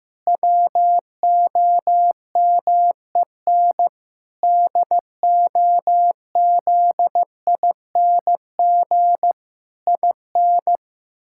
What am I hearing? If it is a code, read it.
Morse code, heard as a keyed tone, about 15 wpm, 700 Hz. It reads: WOMEN DOZING IN
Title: Pangram Challenge v2 - 15wpm